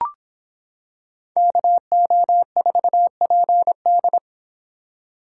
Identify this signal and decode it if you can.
Morse code, heard as a keyed tone, about 26 wpm, 700 Hz. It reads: KO4PD